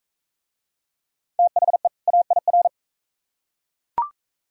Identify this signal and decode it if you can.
Morse code, heard as a keyed tone, about 40 wpm, 700 Hz. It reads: THE AIR